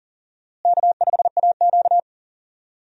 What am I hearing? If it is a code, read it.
Morse code, heard as a keyed tone, about 37 wpm, 700 Hz. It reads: K5AQ
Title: USA Call Signs - 4 Characters - Repeated 2x - 40wpm